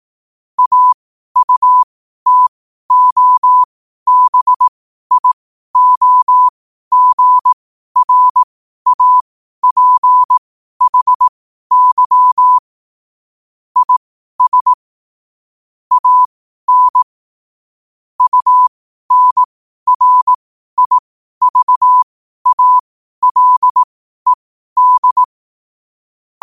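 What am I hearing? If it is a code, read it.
Morse code, heard as a keyed tone, about 18 wpm, spaced as for 12 wpm, 1000 Hz. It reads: AUTOBIOGRAPHY IS AN UNRIVALED
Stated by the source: Morse code quote-of-the-day practice at 12 words per minute.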